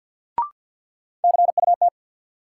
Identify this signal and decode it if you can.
Morse code, heard as a keyed tone, about 53 wpm, 700 Hz. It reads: CUT
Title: Sentences from Top 300 Words - New Word Drill - 50wpm